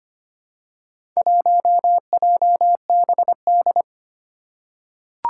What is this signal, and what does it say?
Morse code, heard as a keyed tone, about 25 wpm, 700 Hz. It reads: 1JBD
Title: Four Challenging Letter-Number Combinations 25wpm